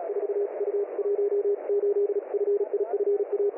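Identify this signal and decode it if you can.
Morse code, heard as a keyed tone, about 36 wpm, 410 Hz, with radio noise. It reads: 4U18FIFA
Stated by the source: Morse received on 10121.6 kHz USB.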